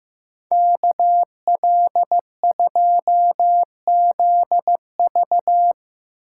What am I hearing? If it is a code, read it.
Morse code, heard as a keyed tone, about 16 wpm, 700 Hz. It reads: KL2ZV
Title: Speed Racing - USA Call Signs 15wpm